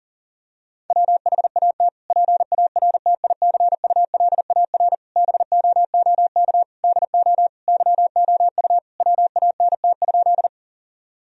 Morse code, heard as a keyed tone, about 40 wpm, 690 Hz. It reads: WHAT PARTICULAR BOOK DO YOU WANT?
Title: Sentences from Top 1000 Words - Review Part 4 - 40wpm